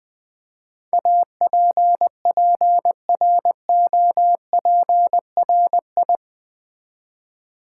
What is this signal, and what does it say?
Morse code, heard as a keyed tone, about 20 wpm, 700 Hz. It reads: APPROPRI
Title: Build A Word - 4 Letters At A Time - 11-Letter Words - 20wpm